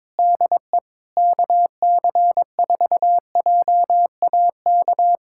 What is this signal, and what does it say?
Morse code, heard as a keyed tone, about 22 wpm, 700 Hz. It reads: DE KC4JAK
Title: K1USN SST QSO Practice 22wpm